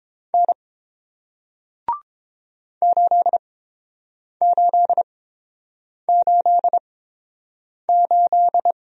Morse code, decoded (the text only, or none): N 8888